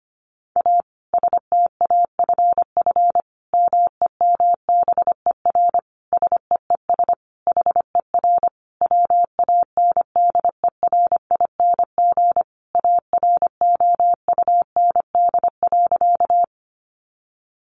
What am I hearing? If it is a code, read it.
Morse code, heard as a keyed tone, about 25 wpm, 700 Hz. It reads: A STAFF MEMBER SEES HER WANDERING AROUND.
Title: Additional Easy Phrases 25wpm